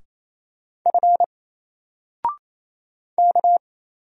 Morse code, heard as a keyed tone, about 27 wpm, 700 Hz. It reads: F K
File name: Single-Letter-Number-28wpm